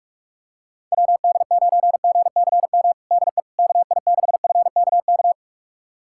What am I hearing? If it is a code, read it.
Morse code, heard as a keyed tone, about 46 wpm, 700 Hz. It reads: WD9GCM DE KI6FKK
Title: USA Callsign DE Callsign - 6 Characters - 45wpm